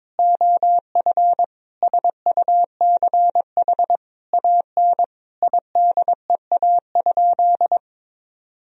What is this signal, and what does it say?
Morse code, heard as a keyed tone, about 22 wpm, 700 Hz. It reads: OF SUCH AN IDEA?
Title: Sentences from Top 400 Words and Abbreviations 22wpm